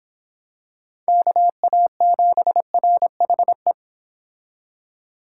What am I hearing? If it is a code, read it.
Morse code, heard as a keyed tone, about 25 wpm, 700 Hz. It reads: KA7RHE